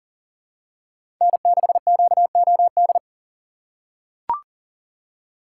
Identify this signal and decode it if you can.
Morse code, heard as a keyed tone, about 40 wpm, 700 Hz. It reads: N6QOD
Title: USA Call Signs - 5 Characters - 2x Character Spacing - 40wpm